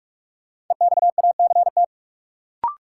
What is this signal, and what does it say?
Morse code, heard as a keyed tone, about 42 wpm, 700 Hz. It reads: EXACT
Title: Sentences from Top 1300 Words - New Word Drill Alt - 45wpm